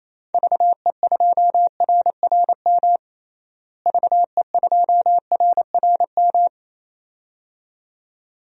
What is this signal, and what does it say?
Morse code, heard as a keyed tone, about 28 wpm, 700 Hz. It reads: VE2RRM VE2RRM
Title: DXCC-100 Call Signs - 6 Characters - Repeated 2x - 28wpm